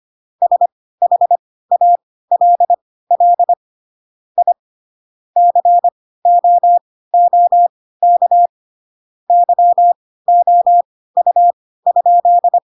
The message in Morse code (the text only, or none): SHALL I COOK YOU?